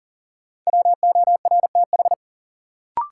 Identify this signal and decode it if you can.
Morse code, heard as a keyed tone, about 40 wpm, 700 Hz. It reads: WORTH
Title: Sentences from Top 800 Words - New Word Drill - 40wpm